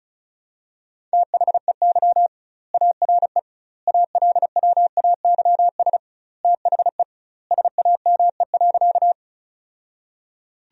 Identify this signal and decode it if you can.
Morse code, heard as a keyed tone, about 34 wpm, 700 Hz. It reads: THEY ARE ALWAYS THE SAME.
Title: 2x Word-Spacing - Sentences from Top 300 Words - 35wpm